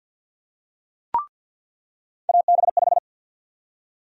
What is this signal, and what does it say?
Morse code, heard as a keyed tone, about 48 wpm, 700 Hz. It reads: AB5
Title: USA Call Sign Prefixes - 50wpm